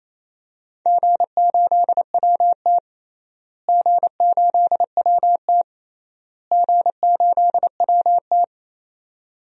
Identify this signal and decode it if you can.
Morse code, heard as a keyed tone, about 28 wpm, 700 Hz. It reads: G8WT G8WT G8WT
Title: DXCC-100 Call Signs - 4 Characters - Repeated 3x - 28wpm